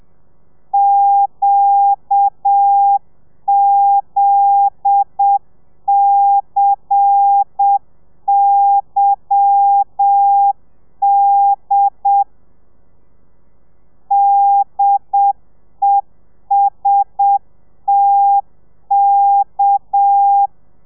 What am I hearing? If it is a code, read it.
Morse code, heard as a keyed tone, about 7 wpm, 800 Hz. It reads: QZCYD DESTK
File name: telegrafia / i2tfj / Lez10 30